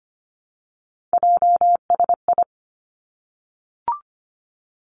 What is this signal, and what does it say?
Morse code, heard as a keyed tone, about 25 wpm, 700 Hz. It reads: JSI
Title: Three Challenging Letter-Number Combinations 25wpm